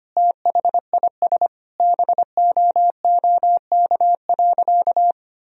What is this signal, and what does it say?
Morse code, heard as a keyed tone, about 25 wpm, 700 Hz. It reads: THIS BOOK.
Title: Sentences from Top 1000 Words - Review Part 4 - 25wpm